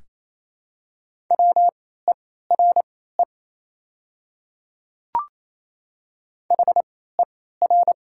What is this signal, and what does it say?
Morse code, heard as a keyed tone, about 28 wpm, spaced as for 16 wpm, 700 Hz. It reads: WERE HER
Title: 3x Character Spacing - Top 100 Words - 28wpm